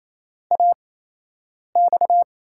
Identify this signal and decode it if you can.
Morse code, heard as a keyed tone, about 28 wpm, spaced as for 8 wpm, 700 Hz. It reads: AX
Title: USA Call Signs - 6 Characters - 8x Character Spacing - 28wpm